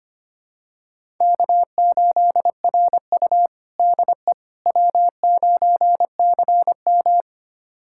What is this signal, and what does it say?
Morse code, heard as a keyed tone, about 25 wpm, 700 Hz. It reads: K8RU DE W9CM
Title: USA Callsign DE Callsign - 4 Characters - 25wpm